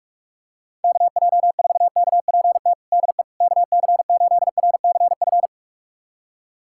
Morse code, heard as a keyed tone, about 46 wpm, 700 Hz. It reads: KJ4KPT DE KC8RCF
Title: USA Callsign DE Callsign - 6 Characters - 45wpm